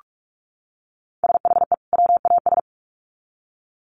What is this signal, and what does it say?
Morse code, heard as a keyed tone, about 44 wpm, 700 Hz. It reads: SHE WAS
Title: Sets of 2 Words 45wpm